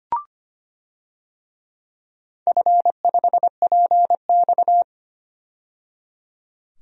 Morse code, heard as a keyed tone, about 25 wpm, 700 Hz. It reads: F5PX
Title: DXCC-100 Call Signs - 4 Characters - 25wpm